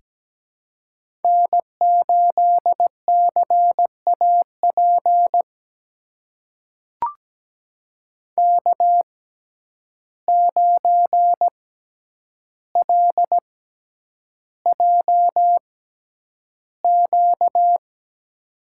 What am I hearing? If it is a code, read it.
Morse code, heard as a keyed tone, about 17 wpm, spaced as for 9 wpm, 700 Hz. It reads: N8CAP K9LJQ
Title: USA Call Signs - 5 Characters - 6x Character Spacing - 17wpm